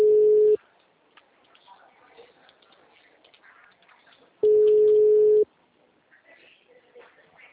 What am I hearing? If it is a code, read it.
Ringback tone (Europe, 425 Hz), with speech.